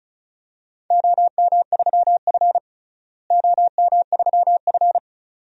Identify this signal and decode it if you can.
Morse code, heard as a keyed tone, about 36 wpm, 700 Hz. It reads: OM3F OM3F
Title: DXCC-100 Call Signs - 4 Characters - Repeated 2x - 35wpm